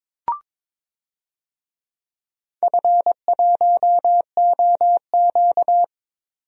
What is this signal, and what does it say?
Morse code, heard as a keyed tone, about 22 wpm, 700 Hz. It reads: F1OQ
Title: DXCC-100 Call Signs - 4 Characters - 22wpm